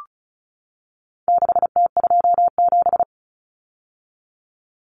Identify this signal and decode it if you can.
Morse code, heard as a keyed tone, about 34 wpm, 700 Hz. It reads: 6T27
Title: Common Cut Numbers - 4-Digit 35wpm